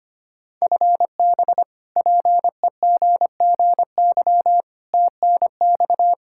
Morse code, heard as a keyed tone, about 25 wpm, 700 Hz. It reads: FB PEGGY TNX
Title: Practice QSO - Exchange 3 - More Info - 25wpm